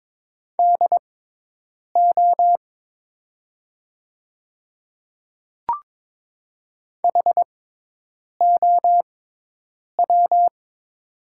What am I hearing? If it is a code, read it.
Morse code, heard as a keyed tone, about 22 wpm, spaced as for 8 wpm, 700 Hz. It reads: DO HOW
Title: 6x Character Spacing - Top 200 Words - 22wpm